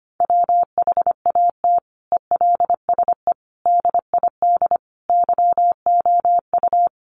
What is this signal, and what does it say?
Morse code, heard as a keyed tone, about 25 wpm, 700 Hz. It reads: WHAT ELSE DID YOU
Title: Sentences from Top 800 Words 25wpm